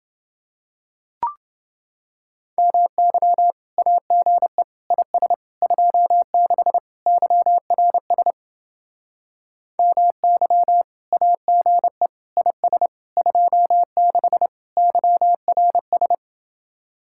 Morse code, heard as a keyed tone, about 31 wpm, 700 Hz. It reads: MY AGE IS 26 YRS MY AGE IS 26 YRS